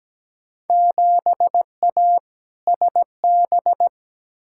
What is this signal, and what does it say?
Morse code, heard as a keyed tone, about 17 wpm, 700 Hz. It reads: 7A SB
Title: Field Day Exchanges Encore - 17wpm